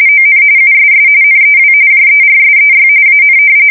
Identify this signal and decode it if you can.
RTTY, about 45.45 baud. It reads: ON USS NEW JERSEY AN
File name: fsk170 / battleshipnj